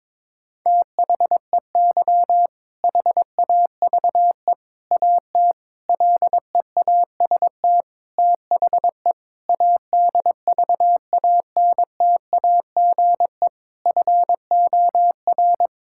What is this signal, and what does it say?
Morse code, heard as a keyed tone, about 22 wpm, 700 Hz. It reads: THEY HAVE AT LEAST THE ADVANTAGE FOR
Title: Sentences from Top 1000 Words - Review Part 6 - 22wpm